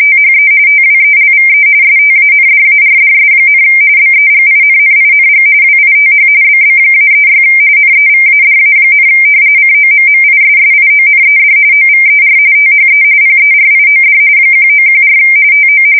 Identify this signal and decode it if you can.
RTTY, about 45.45 baud. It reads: THE TWELVE STANDARDS, BEGINNING WITH BB-36 NEVADA, COMMISSIONED OVER THE PERIOD 1914 TO
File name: fsk170 / battleships